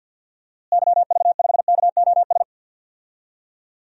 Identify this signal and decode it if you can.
Morse code, heard as a keyed tone, about 49 wpm, 700 Hz. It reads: YV5KOS
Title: DXCC-100 Call Signs - 6 Characters - Repeated 2x - 50wpm